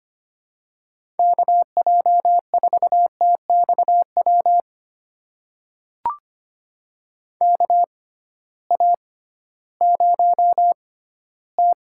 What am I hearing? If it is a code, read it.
Morse code, heard as a keyed tone, about 25 wpm, spaced as for 15 wpm, 700 Hz. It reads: KJ4TXW KA0T
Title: USA Call Signs - 6x Character Spacing - 25wpm